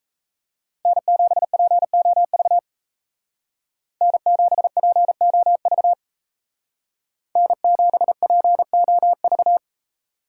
Morse code, heard as a keyed tone, about 35 wpm, 700 Hz. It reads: N7POV N7POV N7POV
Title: Speed Racing - USA Call Signs 28wpm